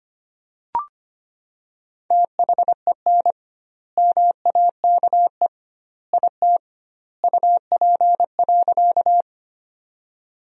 Morse code, heard as a keyed tone, about 25 wpm, 700 Hz. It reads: THEN MAKE IT UP.
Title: 2x Word-Spacing - Sentences from Top 100 Words - 25wpm